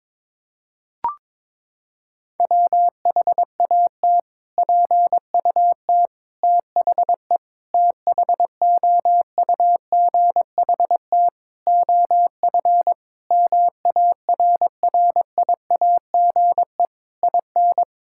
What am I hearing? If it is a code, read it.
Morse code, heard as a keyed tone, about 21 wpm, 700 Hz. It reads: WHAT PUT THE THOUGHT OF MARRIAGE IN
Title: Sentences from Top 1000 Words - Review Part 5 - 22wpm